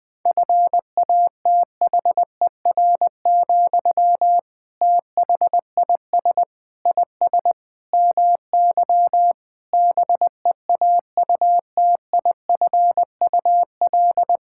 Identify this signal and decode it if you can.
Morse code, heard as a keyed tone, about 19 wpm, 700 Hz. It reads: FATHER, THIS IS MY BEAUTIFUL
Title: Sentences from Top 1000 Words - Review Part 6 - 20wpm